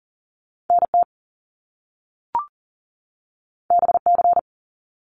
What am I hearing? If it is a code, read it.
Morse code, heard as a keyed tone, about 40 wpm, 700 Hz. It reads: NT BC
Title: Canadian Provinces and Territories 40wpm